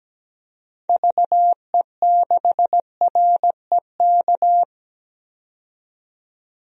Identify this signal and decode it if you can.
Morse code, heard as a keyed tone, about 17 wpm, 700 Hz. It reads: VE6REK
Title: DXCC-100 Call Signs - 6 Characters - 17wpm